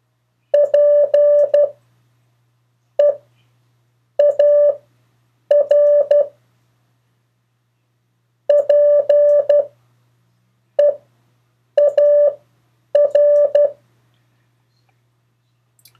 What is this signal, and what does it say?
Morse code, heard as a keyed tone, about 12 wpm, spaced as for 6 wpm, 580 Hz. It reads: PEAR PEAR
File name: 3of4-words-2